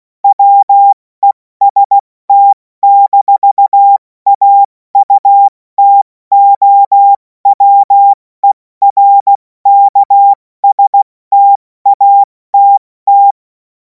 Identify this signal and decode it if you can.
Morse code, heard as a keyed tone, about 16 wpm, 800 Hz. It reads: WEST-AUTOWERKSTATT
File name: Morsekurs / Witze / Witz 21. April 2026 70 F80